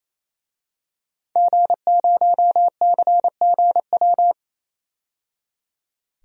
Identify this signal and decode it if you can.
Morse code, heard as a keyed tone, about 28 wpm, 700 Hz. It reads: G0CGW
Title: DXCC-100 Call Signs - 5 Characters - 28wpm